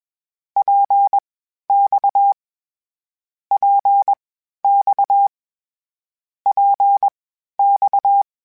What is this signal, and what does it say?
Morse code, heard as a keyed tone, about 21 wpm, spaced as for 12 wpm, 800 Hz. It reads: PX PX PX
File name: Short Runs / Short Runs-mp3s / Balanced @20